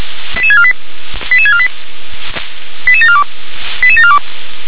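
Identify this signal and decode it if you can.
Five-tone selective call (ZVEI), heard as quick runs of five tones, tones of about 70 ms. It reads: 88648 88648 88742 88742
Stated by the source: ZVEI five-tone alert received by SDR.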